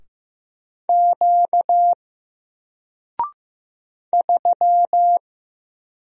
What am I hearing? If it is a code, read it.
Morse code, heard as a keyed tone, about 15 wpm, 700 Hz. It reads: Q 3
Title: Single Letter-Number Alt 15wpm